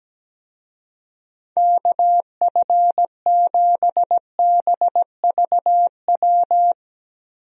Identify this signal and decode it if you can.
Morse code, heard as a keyed tone, about 17 wpm, 700 Hz. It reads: KF7BVW
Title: USA Call Signs - 6 Characters - Repeated 3x - 17wpm